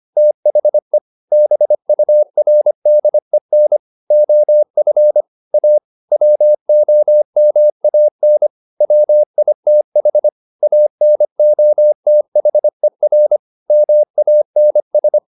Morse code, heard as a keyed tone, about 25 wpm, 600 Hz. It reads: THE BURDEN OF A WOMAN WITH ANOTHER MANS